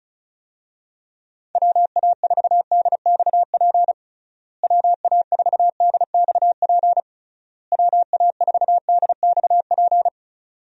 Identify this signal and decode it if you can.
Morse code, heard as a keyed tone, about 36 wpm, 700 Hz. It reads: WA4DXP WA4DXP WA4DXP
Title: USA Call Signs - 6 Characters - Repeated 3x - 35wpm